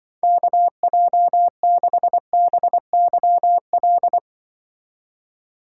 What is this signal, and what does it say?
Morse code, heard as a keyed tone, about 25 wpm, 700 Hz. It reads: KJ6BYL
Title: Speed Racing - USA Call Signs 20wpm